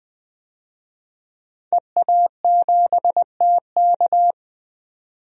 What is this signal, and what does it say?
Morse code, heard as a keyed tone, about 20 wpm, 700 Hz. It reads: EA7TK
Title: DXCC-100 Call Signs - 5 Characters - 20wpm